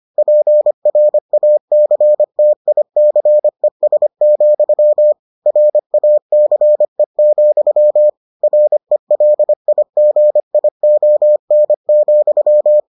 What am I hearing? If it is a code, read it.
Morse code, heard as a keyed tone, about 25 wpm, 600 Hz. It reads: PRACTICES, RACE, RELIGION,